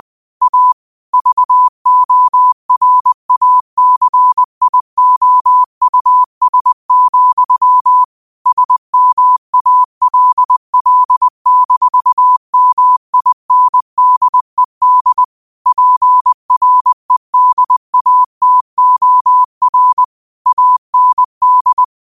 Morse code, heard as a keyed tone, about 20 wpm, 1000 Hz. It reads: A VORACIOUS, SMALL-MINDED PREDATOR AND
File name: qotdfiles / 20wpm